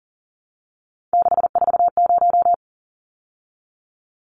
Morse code, heard as a keyed tone, about 40 wpm, 700 Hz. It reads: 640
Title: Numbers - 3-Digit 40wpm